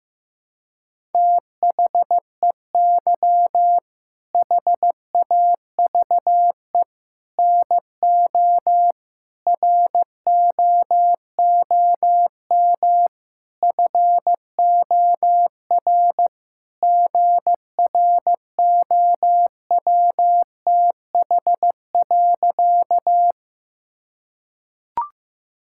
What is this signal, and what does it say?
Morse code, heard as a keyed tone, about 15 wpm, 700 Hz. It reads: THEY HAVE NO ROOM FOR GROWTH.
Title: Sentences from Top 1000 Words - Review Part 5 - 15wpm